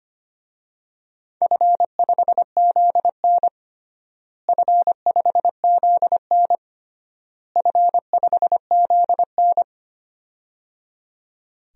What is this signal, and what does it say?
Morse code, heard as a keyed tone, about 25 wpm, 700 Hz. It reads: F5ZN F5ZN F5ZN
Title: DXCC-100 Call Signs - 4 Characters - Repeated 3x - 25wpm